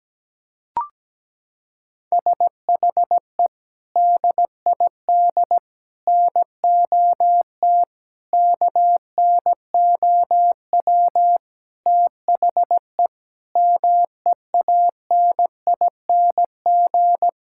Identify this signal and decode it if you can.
Morse code, heard as a keyed tone, about 17 wpm, 700 Hz. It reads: SHE DID NOT KNOW THE MEANING
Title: Sentences from Top 1000 Words - Review Part 6 - 17wpm